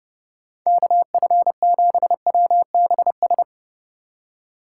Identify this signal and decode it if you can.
Morse code, heard as a keyed tone, about 40 wpm, 700 Hz. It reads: KF7WBS